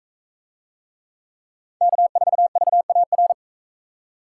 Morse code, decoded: K4VAR

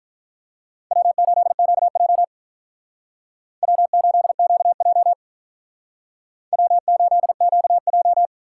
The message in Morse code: W8QJ W8QJ W8QJ

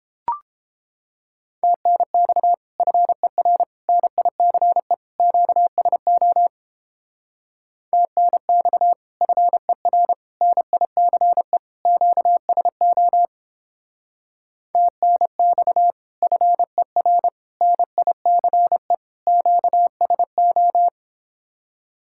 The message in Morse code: TNX FER NICE QSO TNX FER NICE QSO TNX FER NICE QSO

